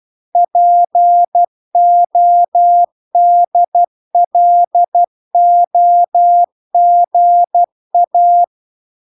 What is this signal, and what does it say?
Morse code, heard as a keyed tone, about 12 wpm, 700 Hz. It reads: PODLOGA